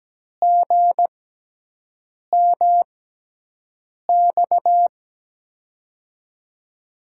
Morse code, heard as a keyed tone, about 17 wpm, spaced as for 6 wpm, 700 Hz. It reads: GMX
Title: USA Call Signs - 6x Character Spacing - 17wpm